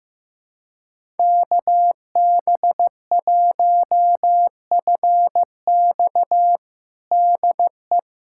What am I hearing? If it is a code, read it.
Morse code, heard as a keyed tone, about 15 wpm, 700 Hz. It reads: KB1FX DE
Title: USA Callsign DE Callsign - 5 Characters - 15wpm